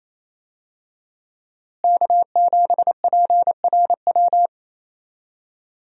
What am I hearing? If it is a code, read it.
Morse code, heard as a keyed tone, about 28 wpm, 700 Hz. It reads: K7PRW
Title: USA Call Signs - 5 Characters - Repeated 2x - 28wpm